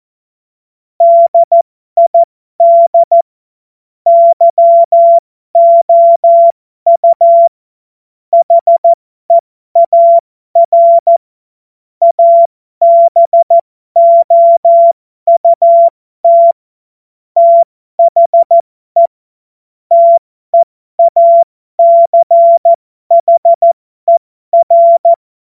Morse code, heard as a keyed tone, about 14 wpm, spaced as for 12 wpm, 680 Hz. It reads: DID YOU HEAR ABOUT THE TEACHER
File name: Morsekurs / Witze